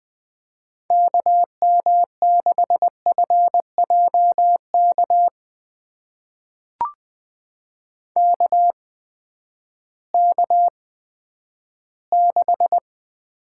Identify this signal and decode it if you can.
Morse code, heard as a keyed tone, about 20 wpm, spaced as for 11 wpm, 700 Hz. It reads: KM6FJK KK6
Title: USA Call Signs - 6 Characters - 8x Character Spacing - 20wpm